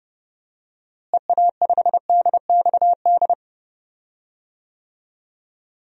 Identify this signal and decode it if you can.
Morse code, heard as a keyed tone, about 30 wpm, 700 Hz. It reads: EA5DXD
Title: DXCC-100 Call Signs - 6 Characters - Repeated 3x - 30wpm